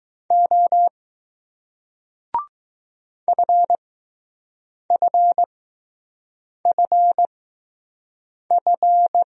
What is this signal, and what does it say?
Morse code, heard as a keyed tone, about 20 wpm, spaced as for 6 wpm, 700 Hz. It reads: O FFFF